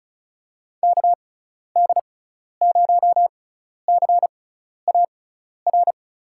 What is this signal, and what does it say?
Morse code, heard as a keyed tone, about 36 wpm, spaced as for 12 wpm, 700 Hz. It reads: KD0CAR
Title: USA Call Signs - 6x Character Spacing - 35wpm